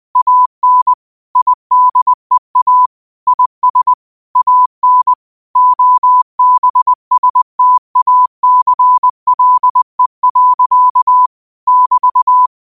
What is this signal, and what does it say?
Morse code, heard as a keyed tone, about 20 wpm, 1000 Hz. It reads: AN IDEA IS AN OBSTACLE. <BT>
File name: qotdfiles / 20wpm